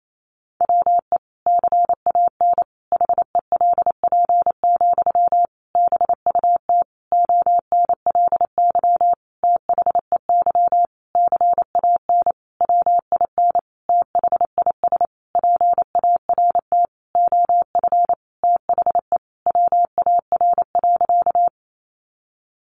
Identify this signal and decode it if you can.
Morse code, heard as a keyed tone, about 28 wpm, 700 Hz. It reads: WE CAN HELP, BUT ONLY THEY CAN WIN THIS PART OF THE WAR.
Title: Sentences from Top 1500 Words 28wpm